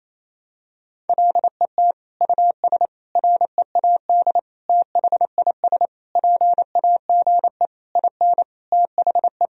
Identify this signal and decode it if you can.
Morse code, heard as a keyed tone, about 28 wpm, 700 Hz. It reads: LET US READ THIS PAGE IN THE
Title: Sentences from Top 1000 Words - Review Part 1 - 28wpm